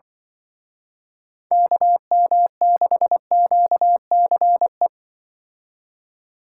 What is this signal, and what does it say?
Morse code, heard as a keyed tone, about 24 wpm, 700 Hz. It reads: KM6QCE
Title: Speed Racing - USA 2x3 Call Signs 20wpm